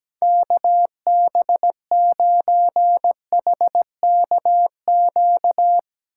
Morse code, heard as a keyed tone, about 17 wpm, 700 Hz. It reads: KB9HKQ
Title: Speed Racing - USA 2x3 Call Signs 17wpm